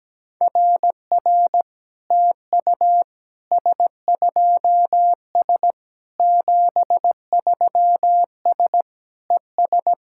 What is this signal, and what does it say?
Morse code, heard as a keyed tone, about 17 wpm, 700 Hz. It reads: RR TU S2S 73S ES
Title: SOTA QSO Practice 17wpm